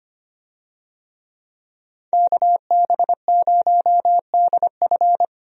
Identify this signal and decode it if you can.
Morse code, heard as a keyed tone, about 25 wpm, 700 Hz. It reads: KB0DF
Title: USA Call Signs - 5 Characters - Repeated 2x - 25wpm